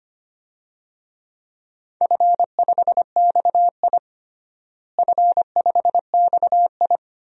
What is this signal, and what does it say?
Morse code, heard as a keyed tone, about 25 wpm, 700 Hz. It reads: F5XI F5XI
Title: DXCC-100 Call Signs - 4 Characters - Repeated 2x - 25wpm